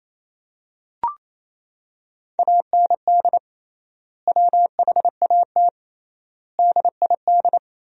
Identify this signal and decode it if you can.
Morse code, heard as a keyed tone, about 27 wpm, 700 Hz. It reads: AND WHAT DID
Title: 3x Word-Spacing - Sentences from Top 300 Words - 28wpm